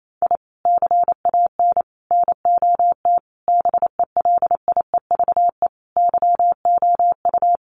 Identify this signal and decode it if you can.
Morse code, heard as a keyed tone, about 28 wpm, 700 Hz. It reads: I CAN NOT BELIEVE YOU
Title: Sets of 5 Words Encore 28wpm